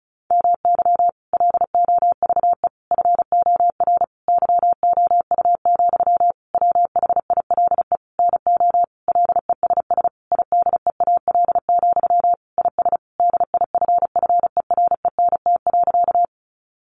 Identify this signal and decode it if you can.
Morse code, heard as a keyed tone, about 34 wpm, 700 Hz. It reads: MY LOVE FOR YOU, WHILE NO LESS IDEAL, IS DIFFERENT.
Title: Sentences from Top 1400 Words 35wpm